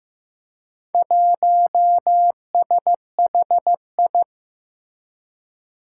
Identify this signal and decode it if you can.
Morse code, heard as a keyed tone, about 15 wpm, 700 Hz. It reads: 1SHI
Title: Four Challenging Letter-Number Combinations 15wpm